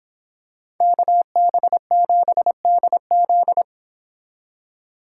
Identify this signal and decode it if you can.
Morse code, heard as a keyed tone, about 26 wpm, 700 Hz. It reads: KB7DZ